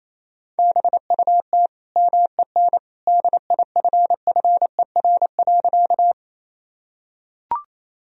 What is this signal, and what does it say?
Morse code, heard as a keyed tone, about 28 wpm, 700 Hz. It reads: BUT MEN DIFFER.
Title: Sentences from Top 1000 Words - Review Part 1 - 28wpm